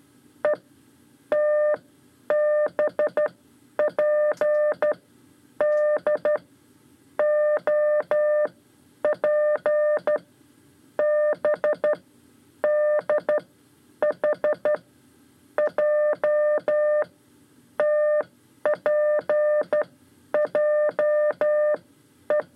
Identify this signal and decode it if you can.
Morse code, heard as a keyed tone, about 12 wpm, spaced as for 8 wpm, 600 Hz. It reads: ETBPD OPBDH JTPJE
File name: 5 Lezione CW 2